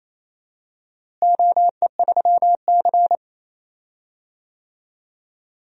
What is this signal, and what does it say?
Morse code, heard as a keyed tone, about 28 wpm, 700 Hz. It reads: OE3C
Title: DXCC-100 Call Signs - 4 Characters - Repeated 2x - 28wpm